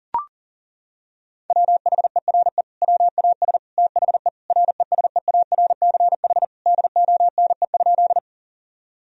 Morse code, heard as a keyed tone, about 40 wpm, 700 Hz. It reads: WHERE WAS THE RESEARCH DONE?
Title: Sentences from Top 1000 Words - Review Part 4 - 40wpm